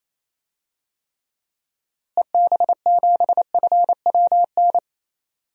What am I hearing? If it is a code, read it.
Morse code, heard as a keyed tone, about 28 wpm, 700 Hz. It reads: EB7FWN
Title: DXCC-100 Call Signs - 6 Characters - Repeated 2x - 28wpm